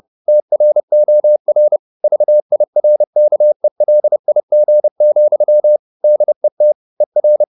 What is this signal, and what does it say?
Morse code, heard as a keyed tone, about 30 wpm, 600 Hz. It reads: TROR VIRKELIG, DET ER
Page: Ferie 30 wpm